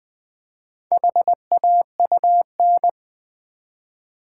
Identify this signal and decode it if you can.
Morse code, heard as a keyed tone, about 20 wpm, 700 Hz. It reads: HAUN
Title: Build A Word - 1 Letter At A Time - 5-Letter Words - 20wpm